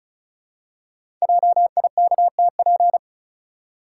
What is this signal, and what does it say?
Morse code, heard as a keyed tone, about 36 wpm, 700 Hz. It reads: JIKTP
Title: Code Groups - 5 Letters - 35wpm